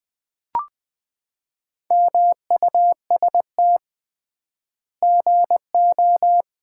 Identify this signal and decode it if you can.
Morse code, heard as a keyed tone, about 20 wpm, 700 Hz. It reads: MUST GO
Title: Common QSO Phrases - 3x Word-Spacing - 20wpm